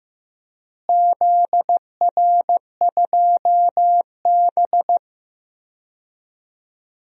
Morse code, heard as a keyed tone, about 16 wpm, 700 Hz. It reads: ZR2B
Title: DXCC-100 Call Signs - 4 Characters - Repeated 2x - 15wpm